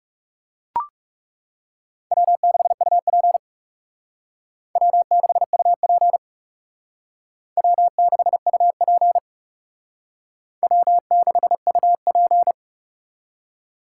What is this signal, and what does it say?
Morse code, heard as a keyed tone, about 38 wpm, 700 Hz. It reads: W6UP W6UP W6UP W6UP